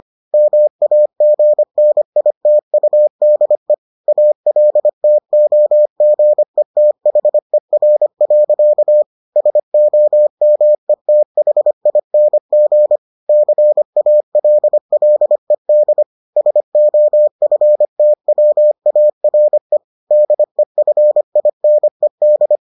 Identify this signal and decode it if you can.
Morse code, heard as a keyed tone, about 25 wpm, 600 Hz. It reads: MAGNITUDE ALTOGETHER. SOMETHING CALLED SOFTWARE DEFINED